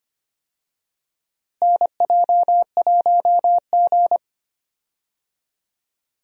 Morse code, heard as a keyed tone, about 25 wpm, 700 Hz. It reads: NJ1G
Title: USA Call Signs - 4 Characters - Repeated 2x - 25wpm